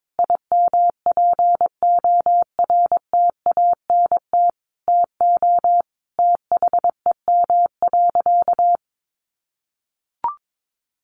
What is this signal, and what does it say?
Morse code, heard as a keyed tone, about 22 wpm, 700 Hz. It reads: IMPORTANT TO THEM.
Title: Sentences from Top 1400 Words 22wpm